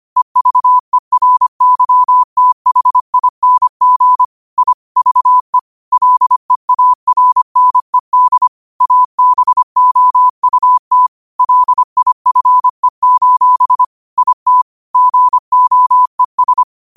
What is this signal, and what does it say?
Morse code, heard as a keyed tone, about 25 wpm, 1000 Hz. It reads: EVERYTHING I VE LEARNED ABOUT LIFE: IT GOES